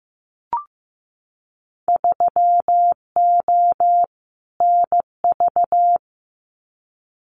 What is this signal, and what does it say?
Morse code, heard as a keyed tone, about 15 wpm, 700 Hz. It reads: 3O NV
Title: Winter Field Day Exchanges Encore 15wpm